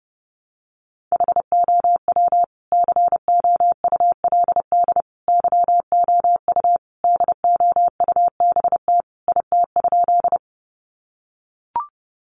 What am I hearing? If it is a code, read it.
Morse code, heard as a keyed tone, about 30 wpm, 700 Hz. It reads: HOW COULD YOU DOUBT IT?
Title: Sentences from Top 1500 Words 30wpm